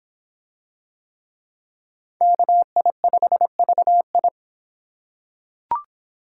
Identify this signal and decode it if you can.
Morse code, heard as a keyed tone, about 25 wpm, 700 Hz. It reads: KI5VI